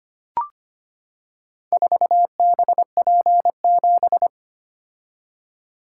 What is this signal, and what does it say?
Morse code, heard as a keyed tone, about 25 wpm, 700 Hz. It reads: 4BP7
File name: Four Letter-Number Combinations - 25wpm